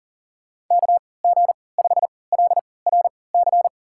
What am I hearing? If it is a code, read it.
Morse code, heard as a keyed tone, about 40 wpm, spaced as for 23 wpm, 700 Hz. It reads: KG5LRC
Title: USA Call Signs - 3x Character Spacing - 40wpm